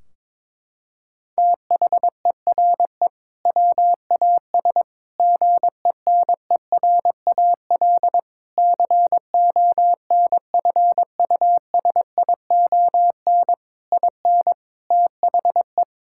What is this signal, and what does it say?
Morse code, heard as a keyed tone, about 22 wpm, 700 Hz. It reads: THERE WAS GENERAL CONFUSION IN THE
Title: Sentences from Top 1600 Words 22wpm